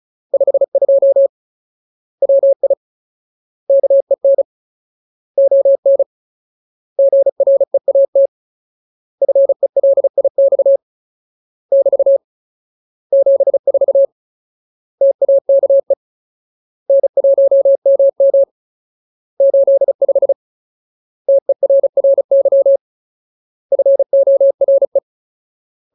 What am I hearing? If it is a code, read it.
Morse code, heard as a keyed tone, about 34 wpm, 550 Hz. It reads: 52 WI KEN ON GREAT FELIX <BT> 74 TAKE N1MM 85 TERRY FORE
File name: Ultimate Word List / 35WPM Ultimate